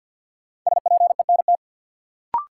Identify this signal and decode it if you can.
Morse code, heard as a keyed tone, about 49 wpm, 700 Hz. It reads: SPENT
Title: Sets of 5 Words - New Word Drill - 50wpm